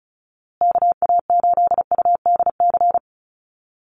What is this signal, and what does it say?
Morse code, heard as a keyed tone, about 34 wpm, 700 Hz. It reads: KA8UDC
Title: USA Call Signs Repeated 2x 35wpm